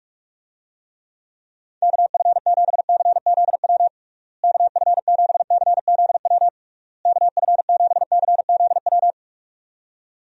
Morse code, heard as a keyed tone, about 44 wpm, 700 Hz. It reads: KF7CZW KF7CZW KF7CZW
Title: USA Call Signs - 6 Characters - Repeated 3x - 45wpm